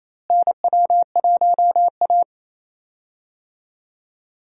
Morse code, heard as a keyed tone, about 27 wpm, 700 Hz. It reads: NW1A
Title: USA Call Signs - 4 Characters - Repeated 3x - 28wpm